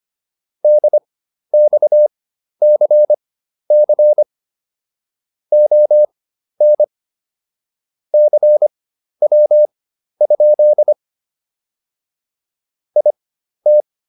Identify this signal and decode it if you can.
Morse code, heard as a keyed tone, about 25 wpm, spaced as for 12 wpm, 600 Hz. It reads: DXCC ON CW? IT